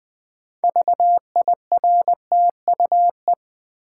Morse code, heard as a keyed tone, about 20 wpm, 700 Hz. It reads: VIRTUE
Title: Build A Word - 3 Letters At A Time - 6-Letter Words - 20wpm